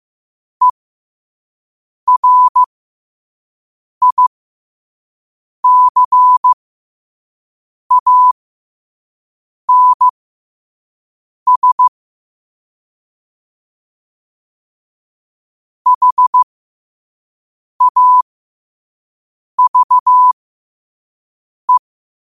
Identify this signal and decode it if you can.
Morse code, heard as a keyed tone, about 15 wpm, spaced as for 5 wpm, 1000 Hz. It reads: ERICANS HAVE
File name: qotdfiles / 5wpm